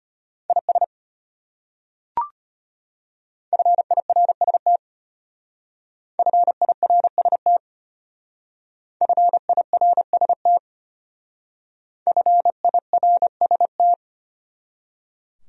Morse code, heard as a keyed tone, about 33 wpm, 700 Hz. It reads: IS FIRST FIRST FIRST FIRST